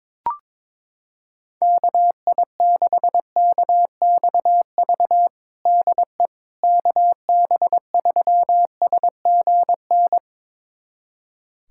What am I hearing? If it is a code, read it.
Morse code, heard as a keyed tone, about 22 wpm, 700 Hz. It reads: KI6KXV DE KB3SGN
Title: USA Callsign DE Callsign 22wpm